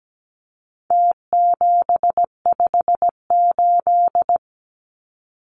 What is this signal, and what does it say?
Morse code, heard as a keyed tone, about 17 wpm, 700 Hz. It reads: T758
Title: Numbers - Contest Style - 4-Digit 17wpm